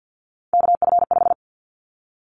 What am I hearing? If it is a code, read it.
Morse code, heard as a keyed tone, about 50 wpm, 700 Hz. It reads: KF5
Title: Callsign Prefixes for Top 25 DXCC - 50wpm